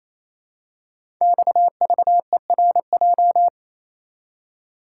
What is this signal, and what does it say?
Morse code, heard as a keyed tone, about 27 wpm, 700 Hz. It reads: XVERJ
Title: Code Groups - 5 Letters - 28wpm